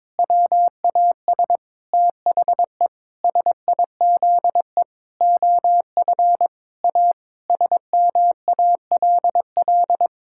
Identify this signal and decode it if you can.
Morse code, heard as a keyed tone, about 22 wpm, 700 Hz. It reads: WAS THE SIZE OF A SMALL
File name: practice / 22wpm / Sentences with 4-Word Phrases - 22wpm